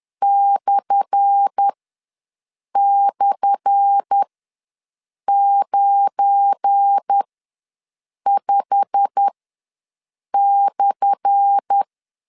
Morse code, heard as a keyed tone, about 10 wpm, spaced as for 6 wpm, 800 Hz. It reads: //95/
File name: LEZIONI / 10 SIGNS / 10A SLASH ws 10wpm 4121